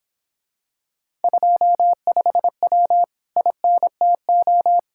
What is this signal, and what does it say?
Morse code, heard as a keyed tone, about 26 wpm, 700 Hz. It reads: 25W INTO